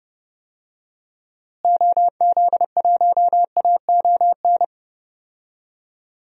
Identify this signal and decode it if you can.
Morse code, heard as a keyed tone, about 30 wpm, 700 Hz. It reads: OZ1AON
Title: DXCC-100 Call Signs - 6 Characters - Repeated 3x - 30wpm